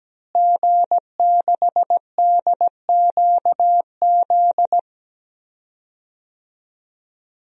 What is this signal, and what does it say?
Morse code, heard as a keyed tone, about 17 wpm, 700 Hz. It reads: G6DQZ
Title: DXCC-100 Call Signs - 5 Characters - Repeated 2x - 17wpm